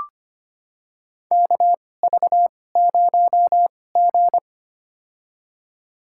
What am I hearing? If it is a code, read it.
Morse code, heard as a keyed tone, about 25 wpm, spaced as for 18 wpm, 700 Hz. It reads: KV0G
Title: USA Call Signs - 4 Characters - 2x Character Spacing - 25wpm